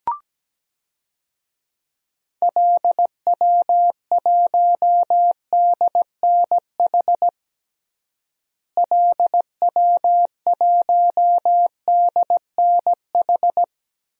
Morse code, heard as a keyed tone, about 17 wpm, 700 Hz. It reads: LW1DNH LW1DNH
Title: DXCC-100 Call Signs - 6 Characters - Repeated 2x - 17wpm